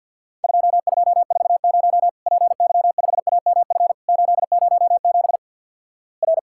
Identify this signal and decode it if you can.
Morse code, heard as a keyed tone, about 48 wpm, 700 Hz. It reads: 2240 PY5AMF 807 R
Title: Worked All Europe DX Contest - QTC Exchange - 50wpm